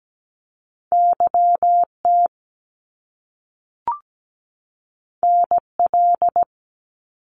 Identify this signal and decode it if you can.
Morse code, heard as a keyed tone, about 17 wpm, 700 Hz. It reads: YT NL
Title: Canadian Provinces and Territories 17wpm